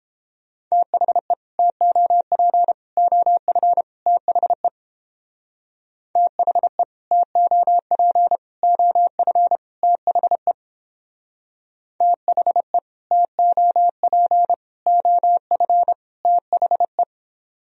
Morse code, heard as a keyed tone, about 31 wpm, 700 Hz. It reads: THE TOP OF THE THE TOP OF THE THE TOP OF THE